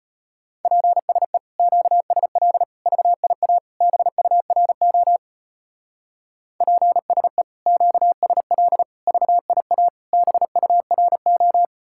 Morse code, heard as a keyed tone, about 29 wpm, 700 Hz. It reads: PSE QSL VIA BURO PSE QSL VIA BURO
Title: Speed-Racing Common QSO Phrases Encore - 25wpm